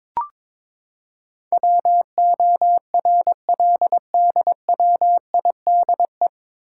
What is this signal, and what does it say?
Morse code, heard as a keyed tone, about 21 wpm, 700 Hz. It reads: WORLDWIDE
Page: Morse Code Ninja practice 22wpm